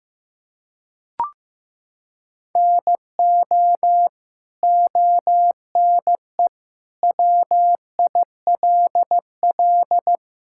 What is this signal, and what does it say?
Morse code, heard as a keyed tone, about 15 wpm, 700 Hz. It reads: NO ONE WILL
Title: Sentences from Top 1000 Words - Review Part 5 - 15wpm